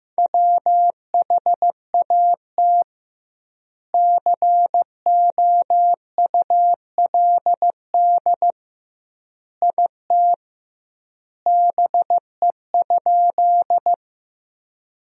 Morse code, heard as a keyed tone, about 15 wpm, 700 Hz. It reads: WHAT COULD IT BE?
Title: 2x Word-Spacing - Sentences from Top 100 Words - 15wpm